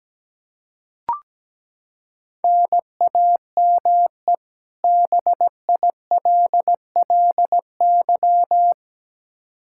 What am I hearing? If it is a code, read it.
Morse code, heard as a keyed tone, about 17 wpm, 700 Hz. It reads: NAME BILLY
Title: Practice QSO - Exchange 1 - Beginning - 17wpm